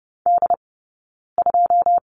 Morse code, heard as a keyed tone, about 30 wpm, 700 Hz. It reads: D 2
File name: 092 Lesson 16d - Letter c - Two Characters 30wpm